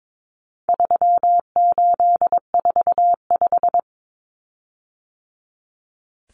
Morse code, heard as a keyed tone, about 22 wpm, 700 Hz. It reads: 3845
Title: Common Cut Numbers - 4-Digit 22wpm